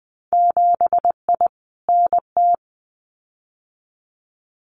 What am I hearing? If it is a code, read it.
Morse code, heard as a keyed tone, about 20 wpm, 700 Hz. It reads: 7I NT
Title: Winter Field Day Exchanges Encore 20wpm